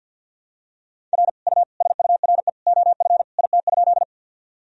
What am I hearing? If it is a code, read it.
Morse code, heard as a keyed tone, about 51 wpm, 700 Hz. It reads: R U SURE OF IT?
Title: Sentences from Top 300 Words and Abbreviations 50wpm